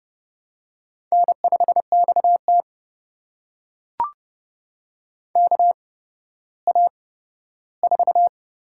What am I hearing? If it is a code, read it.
Morse code, heard as a keyed tone, about 31 wpm, spaced as for 15 wpm, 700 Hz. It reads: N5XT KA4
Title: USA Call Signs - 8x Character Spacing - 30wpm